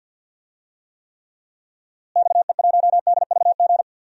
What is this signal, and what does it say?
Morse code, heard as a keyed tone, about 46 wpm, 700 Hz. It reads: XE1DVG
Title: DXCC-100 Call Signs - 6 Characters - Repeated 3x - 50wpm